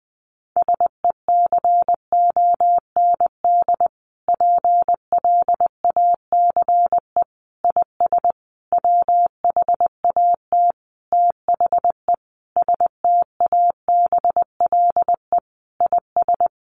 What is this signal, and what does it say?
Morse code, heard as a keyed tone, about 20 wpm, 700 Hz. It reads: SECOND PLACE IS WHAT THE STABLE IS
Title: Sentences from Top 1500 Words 20wpm